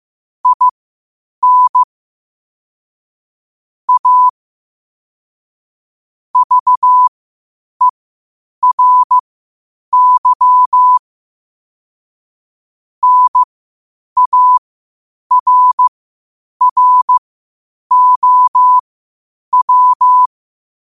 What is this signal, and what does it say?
Morse code, heard as a keyed tone, about 15 wpm, spaced as for 8 wpm, 1000 Hz. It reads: IN A VERY NARROW
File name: qotdfiles / 8wpm